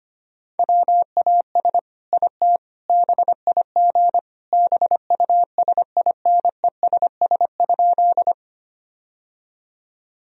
Morse code, heard as a keyed tone, about 25 wpm, 700 Hz. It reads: WAS IT BIG BUSINESS?
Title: Sentences from Top 1000 Words - Review Part 1 - 25wpm